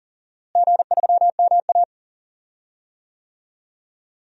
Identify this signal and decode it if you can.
Morse code, heard as a keyed tone, about 40 wpm, 700 Hz. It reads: G3MA
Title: DXCC-100 Call Signs - 4 Characters - Repeated 3x - 40wpm